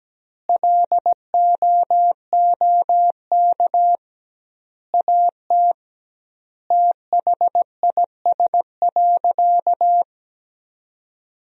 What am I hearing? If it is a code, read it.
Morse code, heard as a keyed tone, about 17 wpm, 700 Hz. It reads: LOOK AT THIS.
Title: 2x Word-Spacing - Sentences from Top 300 Words - 17wpm